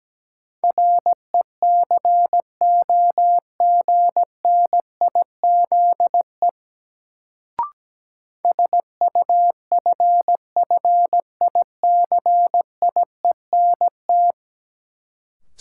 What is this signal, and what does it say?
Morse code, heard as a keyed tone, about 17 wpm, 700 Hz. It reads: RECOGNIZE SUFFICIENT
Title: Sentences from Top 1400 Words - New Word Drill Alt - 17wpm